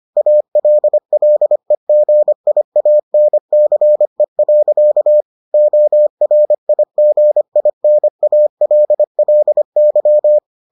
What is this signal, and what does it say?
Morse code, heard as a keyed tone, about 25 wpm, 600 Hz. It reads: ALLEGIANCE. ORIGINALLY